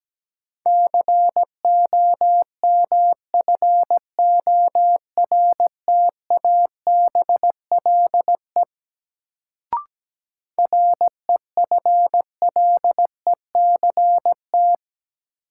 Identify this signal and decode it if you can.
Morse code, heard as a keyed tone, about 17 wpm, 700 Hz. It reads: COMFORTABLE REFLECT
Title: Sentences from Top 1500 Words - New Word Drill Alt - 17wpm